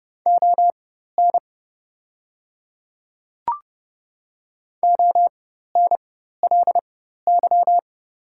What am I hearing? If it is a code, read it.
Morse code, heard as a keyed tone, about 29 wpm, spaced as for 14 wpm, 700 Hz. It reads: ON ONLY
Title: 4x Character Spacing - Top 300 Words - 30wpm